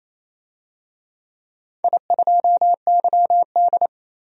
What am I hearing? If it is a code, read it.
Morse code, heard as a keyed tone, about 28 wpm, 700 Hz. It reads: I2YD
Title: DXCC-100 Call Signs - 4 Characters - 28wpm